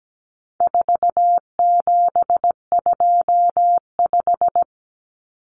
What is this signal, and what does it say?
Morse code, heard as a keyed tone, about 17 wpm, 700 Hz. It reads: 4725
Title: Numbers - 4-Digit 17wpm